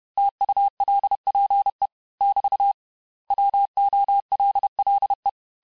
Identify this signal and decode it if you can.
Morse code, heard as a keyed tone, about 31 wpm, 800 Hz. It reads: TULPE <BT> WOLLE
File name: cours-du-mercredi / audios / cours-861-texte-les-mots-vitesse-29MPM